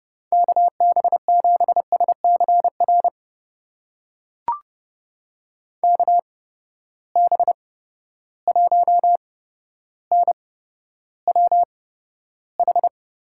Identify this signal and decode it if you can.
Morse code, heard as a keyed tone, about 31 wpm, spaced as for 13 wpm, 700 Hz. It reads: KB7SCR KB1NWH
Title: USA Call Signs - 6 Characters - 8x Character Spacing - 30wpm